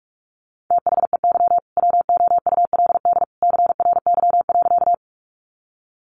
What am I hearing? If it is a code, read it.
Morse code, heard as a keyed tone, about 44 wpm, 700 Hz. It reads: THEY WOULD CRY.
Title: Sentences from Top 500 Words 45wpm